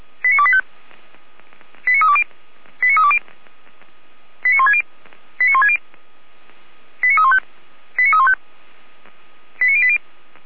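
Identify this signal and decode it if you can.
Five-tone selective call (ZVEI), heard as quick runs of five tones, tones of about 70 ms. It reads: 89285 89320 89320 89160 89160 89315 89315 89080
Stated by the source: ZVEI five-tone call received by SDR.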